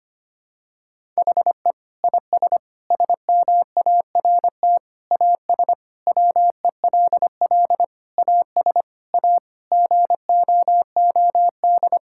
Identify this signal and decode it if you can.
Morse code, heard as a keyed tone, about 25 wpm, 700 Hz. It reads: HE IS SMART AS WELL AS A GOOD
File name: practice / 25wpm / Sentences with 4-Word Phrases - 25wpm